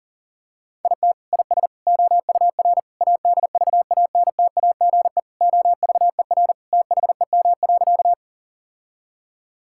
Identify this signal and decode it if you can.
Morse code, heard as a keyed tone, about 40 wpm, 700 Hz. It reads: IT IS OUR ADVANTAGE OVER THEM.
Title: Sentences from Top 1000 Words - Review Part 6 - 40wpm